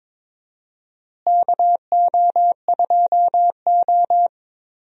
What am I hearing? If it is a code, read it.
Morse code, heard as a keyed tone, about 23 wpm, 700 Hz. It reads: KO2O
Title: USA Call Signs - 4 Characters - Repeated 3x - 22wpm